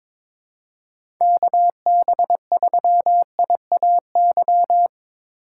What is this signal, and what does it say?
Morse code, heard as a keyed tone, about 21 wpm, 700 Hz. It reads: KB3IAY
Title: USA Call Signs - 6 Characters - 2x Character Spacing - 22wpm